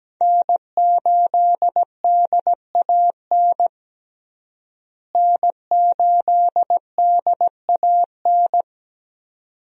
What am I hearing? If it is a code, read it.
Morse code, heard as a keyed tone, about 17 wpm, 700 Hz. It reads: N8DAN N8DAN
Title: USA Call Signs - 5 Characters - Repeated 2x - 17wpm